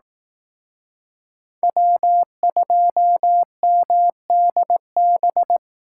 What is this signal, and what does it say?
Morse code, heard as a keyed tone, about 18 wpm, 700 Hz. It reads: W2MDB
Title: Speed Racing - USA 1x3 Call Signs 15wpm